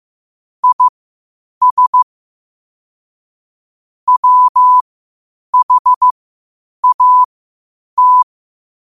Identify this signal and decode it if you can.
Morse code, heard as a keyed tone, about 15 wpm, spaced as for 8 wpm, 1000 Hz. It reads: IS WHAT
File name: qotdfiles / 8wpm